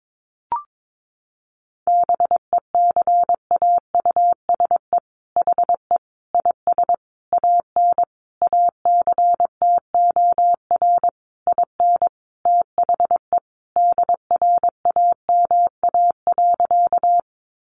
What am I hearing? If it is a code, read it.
Morse code, heard as a keyed tone, about 22 wpm, 700 Hz. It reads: BECAUSE HE IS AN ACTOR IN THE DRAMA.
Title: Sentences from Top 1500 Words 22wpm